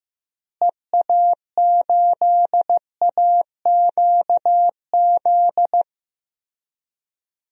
Morse code, heard as a keyed tone, about 15 wpm, 700 Hz. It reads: EA8AQZ
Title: DXCC-100 Call Signs - 6 Characters - Repeated 2x - 15wpm